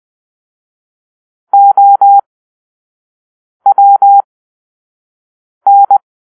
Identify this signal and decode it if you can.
Morse code, heard as a keyed tone, about 20 wpm, spaced as for 5 wpm, 800 Hz. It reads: OWN